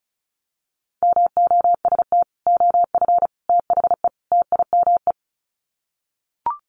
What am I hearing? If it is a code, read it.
Morse code, heard as a keyed tone, about 34 wpm, 700 Hz. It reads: MOST OF THE TIME
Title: Sets of 4 Words 35wpm